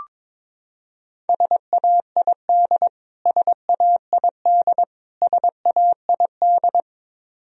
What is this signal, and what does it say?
Morse code, heard as a keyed tone, about 22 wpm, 700 Hz. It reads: SAID SAID SAID
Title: Top 100 Words Repeated 22wpm